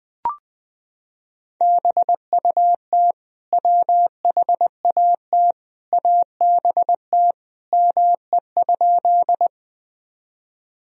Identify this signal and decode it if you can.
Morse code, heard as a keyed tone, about 20 wpm, 700 Hz. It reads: BUT WHAT ABT ME?
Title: Sentences from Top 200 Words and Abbreviations 20wpm